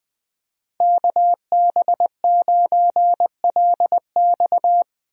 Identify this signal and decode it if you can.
Morse code, heard as a keyed tone, about 20 wpm, 700 Hz. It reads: KB9LX
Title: USA Call Signs - 5 Characters - Repeated 2x - 20wpm